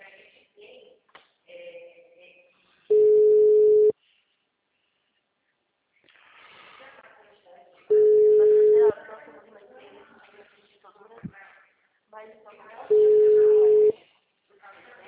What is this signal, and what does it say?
Ringback tone (Europe, 425 Hz), with speech.